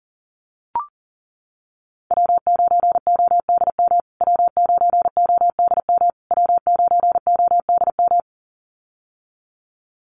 Morse code, heard as a keyed tone, about 40 wpm, 700 Hz. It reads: W9ODM W9ODM W9ODM
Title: USA Call Signs Repeated 3x 40wpm